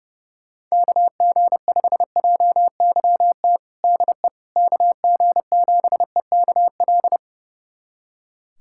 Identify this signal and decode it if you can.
Morse code, heard as a keyed tone, about 30 wpm, 700 Hz. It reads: KG5JYT DE KG7EKL
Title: USA Callsign DE Callsign 30wpm